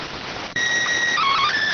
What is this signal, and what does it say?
SSTV (slow-scan television), PD 290.